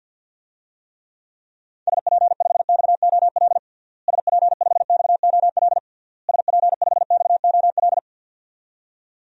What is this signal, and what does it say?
Morse code, heard as a keyed tone, about 46 wpm, 700 Hz. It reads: SP5XOL SP5XOL SP5XOL
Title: DXCC-100 Call Signs - 6 Characters - Repeated 3x - 50wpm